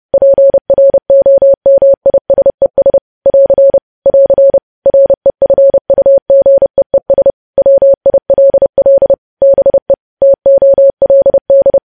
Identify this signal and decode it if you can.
Morse code, heard as a keyed tone, about 30 wpm, 550 Hz. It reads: PROMISES <AR> <AR> REFUGEES WILL BE TOLD
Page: news-headlines-in-morse-code-at-30-wpm